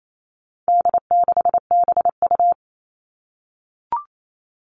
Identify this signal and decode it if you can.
Morse code, heard as a keyed tone, about 28 wpm, 700 Hz. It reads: D6BU
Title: All Cut Numbers - 4-Digit 28wpm